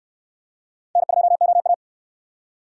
Morse code, heard as a keyed tone, about 63 wpm, 700 Hz. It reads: N2OA